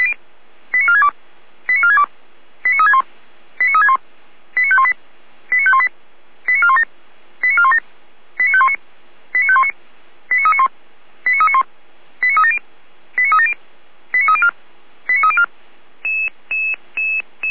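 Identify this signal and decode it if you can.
Five-tone selective call (ZVEI), heard as quick runs of five tones, tones of about 70 ms. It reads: 89462 89462 89361 89361 89518 89518 89417 89417 89519 89519 89391 89391 89370 89370 89304 89304
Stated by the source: ZVEI five-tone call 89505 received by SDR, from feuerwehr-orlamuende.org fire brigade audio server.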